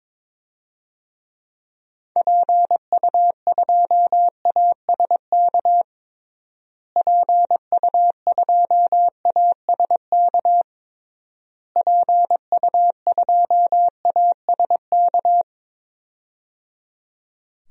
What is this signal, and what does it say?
Morse code, heard as a keyed tone, about 22 wpm, 700 Hz. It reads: PU2ASK PU2ASK PU2ASK
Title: DXCC-100 Call Signs - 6 Characters - Repeated 3x - 22wpm